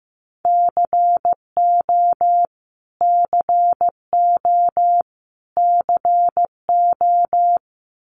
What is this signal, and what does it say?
Morse code, heard as a keyed tone, about 15 wpm, 700 Hz. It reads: CO CO CO
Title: US States Repeated 3x 15wpm